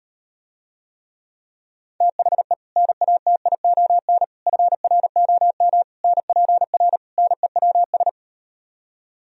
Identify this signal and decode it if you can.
Morse code, heard as a keyed tone, about 39 wpm, 700 Hz. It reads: THE NATION FROM NPR NEWS